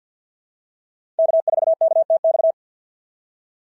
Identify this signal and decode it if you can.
Morse code, heard as a keyed tone, about 52 wpm, 690 Hz. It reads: K4KTX
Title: SOTA QSO Practice 50wpm